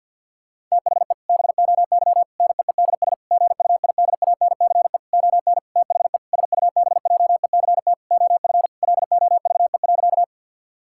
Morse code, heard as a keyed tone, about 50 wpm, 700 Hz. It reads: THE BOY NEEDS GUIDANCE ON THE SUBJECT OF LOVE.
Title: Sentences from Top 1600 Words 50wpm